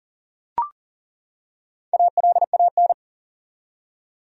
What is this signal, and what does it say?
Morse code, heard as a keyed tone, about 40 wpm, 700 Hz. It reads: ALAN
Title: USA Common Names - Max 5 Letters - 40wpm